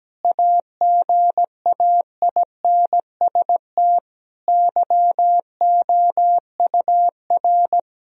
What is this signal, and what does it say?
Morse code, heard as a keyed tone, about 17 wpm, 700 Hz. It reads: AGAINST YOUR
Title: Sentences from Top 1000 Words - Review Part 3 - 17wpm